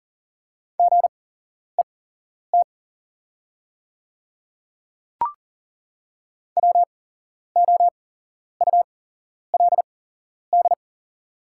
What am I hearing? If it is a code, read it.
Morse code, heard as a keyed tone, about 37 wpm, spaced as for 11 wpm, 700 Hz. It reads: GET WOULD
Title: 8x Character Spacing - Top 300 Words - 40wpm